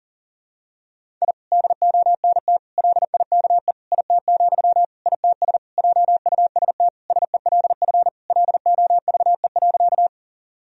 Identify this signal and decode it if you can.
Morse code, heard as a keyed tone, about 40 wpm, 700 Hz. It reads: I DONT LIKE IT, ITS JUST SELF LOVE.
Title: Sentences from Top 1000 Words - Review Part 1 - 40wpm